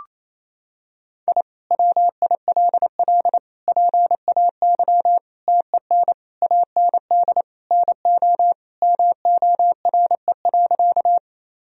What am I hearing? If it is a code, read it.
Morse code, heard as a keyed tone, about 28 wpm, 700 Hz. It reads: I WILL PAY TEN AND NO MORE.
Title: Sentences from Top 1000 Words - Review Part 2 - 28wpm